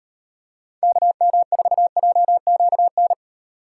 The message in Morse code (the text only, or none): KM4JQN